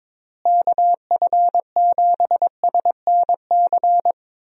Morse code, heard as a keyed tone, about 22 wpm, 700 Hz. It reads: KF7SNC
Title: USA Call Signs - 6 Characters - Repeated 2x - 22wpm